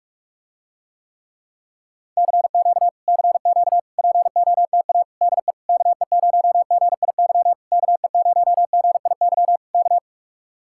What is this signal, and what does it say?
Morse code, heard as a keyed tone, about 49 wpm, 700 Hz. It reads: CQ CQ POTA DE KE0GIY KE0GIY K